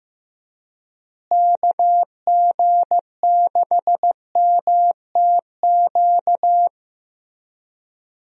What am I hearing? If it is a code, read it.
Morse code, heard as a keyed tone, about 15 wpm, 700 Hz. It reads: KG6MTQ
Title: USA Call Signs - 6 Characters - Repeated 3x - 15wpm